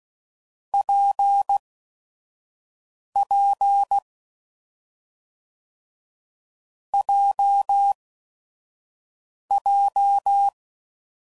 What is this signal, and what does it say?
Morse code, heard as a keyed tone, about 16 wpm, spaced as for 5 wpm, 800 Hz. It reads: PP JJ
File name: Combinations / Difficult Combos / AWJ1